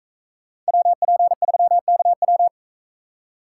Morse code, heard as a keyed tone, about 41 wpm, 700 Hz. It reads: WP3KW